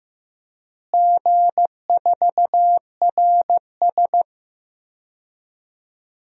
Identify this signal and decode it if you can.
Morse code, heard as a keyed tone, about 15 wpm, 700 Hz. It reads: G4RS
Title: DXCC-100 Call Signs - 4 Characters - Repeated 2x - 15wpm